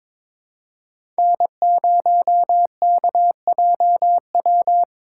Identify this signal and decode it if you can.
Morse code, heard as a keyed tone, about 22 wpm, 700 Hz. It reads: N0KJW
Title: USA Call Signs - 5 Characters - 3x Character Spacing - 22wpm